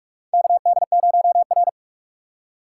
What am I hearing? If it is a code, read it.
Morse code, heard as a keyed tone, about 44 wpm, 700 Hz. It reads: KD0R
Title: USA Call Signs - 6x Character Spacing - 45wpm